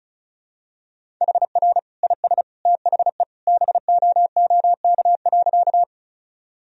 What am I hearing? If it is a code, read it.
Morse code, heard as a keyed tone, about 34 wpm, 700 Hz. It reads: HR IS THE BOOK.
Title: Sentences from Top 300 Words and Abbreviations 35wpm